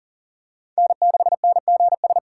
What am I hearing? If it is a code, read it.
Morse code, heard as a keyed tone, about 40 wpm, 700 Hz. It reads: N6NGS
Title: USA Call Signs - 6x Character Spacing - 40wpm